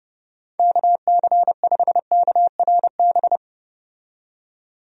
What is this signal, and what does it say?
Morse code, heard as a keyed tone, about 30 wpm, 700 Hz. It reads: KC5KRB
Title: USA Call Signs - 6 Characters - 30wpm